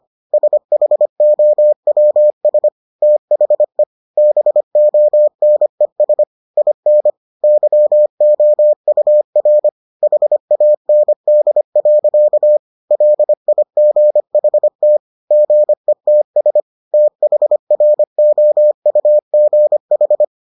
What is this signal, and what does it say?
Morse code, heard as a keyed tone, about 25 wpm, 600 Hz. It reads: SHOWS THE BONES IN YOUR HAND. LIGHT GETS THROUGH